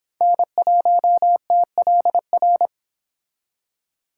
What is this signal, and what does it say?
Morse code, heard as a keyed tone, about 31 wpm, 700 Hz. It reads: N1TLR